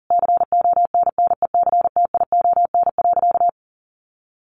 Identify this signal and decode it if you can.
Morse code, heard as a keyed tone, about 40 wpm, 700 Hz. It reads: CONNECTION.
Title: Sentences from Top 1300 Words 40wpm